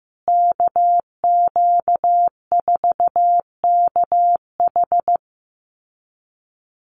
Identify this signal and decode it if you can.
Morse code, heard as a keyed tone, about 15 wpm, 700 Hz. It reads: KQ4KH
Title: USA Call Signs Repeated 2x 15wpm